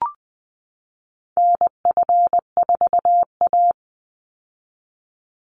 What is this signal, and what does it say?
Morse code, heard as a keyed tone, about 20 wpm, 700 Hz. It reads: NF4A
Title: Call Signs - 20wpm